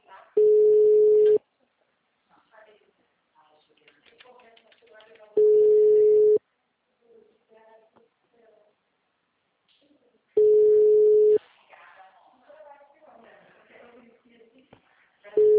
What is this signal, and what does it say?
Ringback tone (Europe, 425 Hz), with speech.